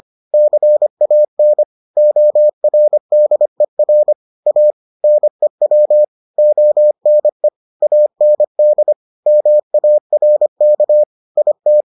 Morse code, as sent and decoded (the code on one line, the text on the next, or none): -.-. .- -. / --- .-. -.. .-. / .- / -. .-- / --- -. / .- -. -.. / -- .- .-. -.- / .. -
CAN ORDER A NEW ONE AND MARK IT